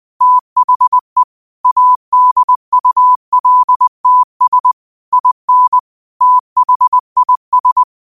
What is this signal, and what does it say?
Morse code, heard as a keyed tone, about 20 wpm, 1000 Hz. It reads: THE ADULTS IN THIS
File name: qotdfiles / 20wpm